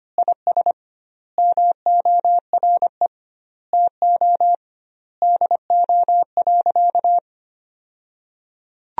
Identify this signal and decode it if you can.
Morse code, heard as a keyed tone, about 25 wpm, 700 Hz. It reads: IS MORE TO DO.
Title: 2x Word-Spacing - Sentences from Top 100 Words - 25wpm